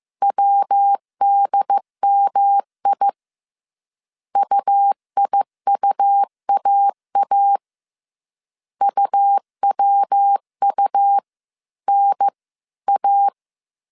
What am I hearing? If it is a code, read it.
Morse code, heard as a keyed tone, about 15 wpm, 800 Hz. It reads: WDMI UIUAA UWU N A